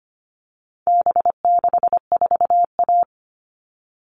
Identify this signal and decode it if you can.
Morse code, heard as a keyed tone, about 25 wpm, 700 Hz. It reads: B64A
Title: All Cut Numbers - 4-Digit 25wpm